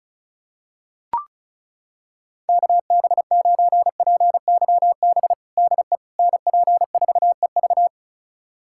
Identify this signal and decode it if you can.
Morse code, heard as a keyed tone, about 36 wpm, 700 Hz. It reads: KB9PYB DE NP4EV
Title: USA Callsign DE Callsign 35wpm